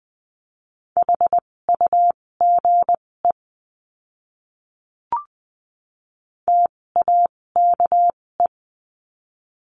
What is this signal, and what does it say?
Morse code, heard as a keyed tone, about 20 wpm, spaced as for 16 wpm, 700 Hz. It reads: HUGE TAKE
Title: 256 Lesson 42a - Words 12-20wpm